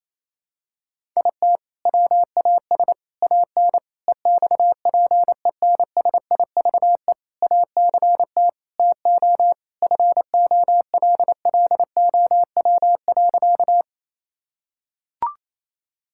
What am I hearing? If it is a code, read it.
Morse code, heard as a keyed tone, about 28 wpm, 700 Hz. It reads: IT WAS AN EXPENSIVE ACT TO FOLLOW.
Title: Sentences from Top 1000 Words - Review Part 6 - 28wpm